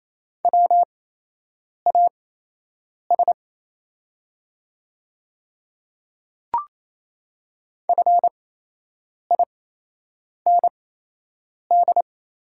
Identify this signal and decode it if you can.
Morse code, heard as a keyed tone, about 28 wpm, spaced as for 8 wpm, 700 Hz. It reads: WAS FIND
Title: 8x Character Spacing - Top 100 Words - 28wpm